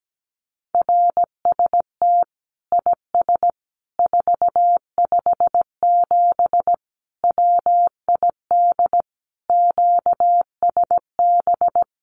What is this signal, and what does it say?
Morse code, heard as a keyed tone, about 17 wpm, 700 Hz. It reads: RST IS 457 WID QSB
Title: Common QSO Phrases 17wpm